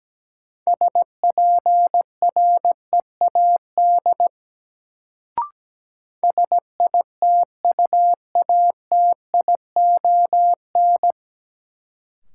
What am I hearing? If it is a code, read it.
Morse code, heard as a keyed tone, about 17 wpm, 700 Hz. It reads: SPREAD SITUATION
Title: Sentences from Top 1200 Words - New Word Drill Alt - 17wpm